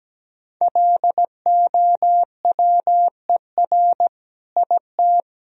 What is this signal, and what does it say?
Morse code, heard as a keyed tone, about 17 wpm, 700 Hz. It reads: LOWER IT
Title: Sentences from Top 1000 Words - Review Part 4 - 17wpm